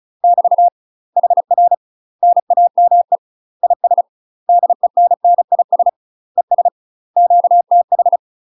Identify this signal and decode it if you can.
Morse code, heard as a keyed tone, about 33 wpm, 700 Hz. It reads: <BT> HR NAME IS DENNIS ES QTH